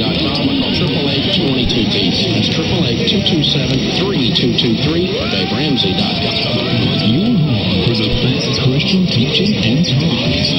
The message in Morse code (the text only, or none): none